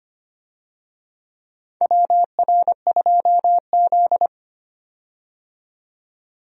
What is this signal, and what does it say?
Morse code, heard as a keyed tone, about 25 wpm, 700 Hz. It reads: WR2Z
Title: USA Call Signs - 4 Characters - 25wpm